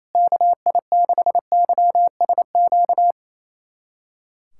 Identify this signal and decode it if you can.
Morse code, heard as a keyed tone, about 27 wpm, 700 Hz. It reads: KI6YSQ
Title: USA Callsign DE Callsign - 6 Characters - 28wpm